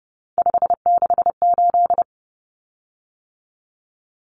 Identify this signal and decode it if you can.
Morse code, heard as a keyed tone, about 30 wpm, 700 Hz. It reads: 568
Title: Common Cut Numbers - 3-Digit 30wpm